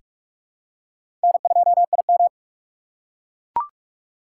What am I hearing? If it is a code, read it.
Morse code, heard as a keyed tone, about 44 wpm, 700 Hz. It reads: N2IM
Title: USA Call Signs - 4x Character Spacing - 45wpm